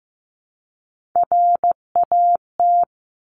Morse code, heard as a keyed tone, about 15 wpm, 700 Hz. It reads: RAT
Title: Top Three Letter Combinations in Words 15wpm